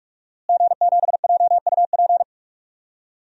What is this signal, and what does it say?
Morse code, heard as a keyed tone, about 45 wpm, 700 Hz. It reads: G7JUP
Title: DXCC-100 Call Signs - 5 Characters - Repeated 2x - 45wpm